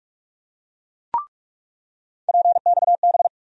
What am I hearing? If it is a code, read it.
Morse code, heard as a keyed tone, about 41 wpm, 700 Hz. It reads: PXB
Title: USA Call Sign Postfixes - 3 Letters 45wpm